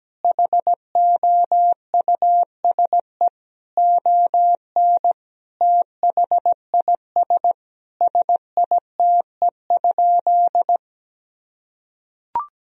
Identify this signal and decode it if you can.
Morse code, heard as a keyed tone, about 17 wpm, 700 Hz. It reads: HOUSE ON THIS SITE?
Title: Sentences from Top 1000 Words - Review Part 4 - 17wpm